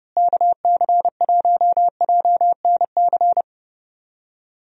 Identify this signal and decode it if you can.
Morse code, heard as a keyed tone, about 32 wpm, 700 Hz. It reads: KC1JNC
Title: USA Call Signs - 6 Characters - Repeated 2x - 30wpm